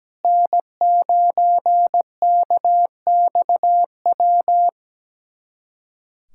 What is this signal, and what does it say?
Morse code, heard as a keyed tone, about 17 wpm, 700 Hz. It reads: N9KXW
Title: USA Callsign DE Callsign - 5 Characters - 17wpm